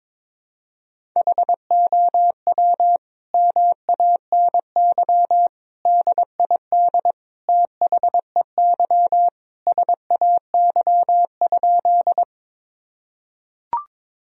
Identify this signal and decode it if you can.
Morse code, heard as a keyed tone, about 22 wpm, 700 Hz. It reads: HOW MANY DID THEY SAY?
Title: Sentences from Top 1000 Words - Review Part 1 - 22wpm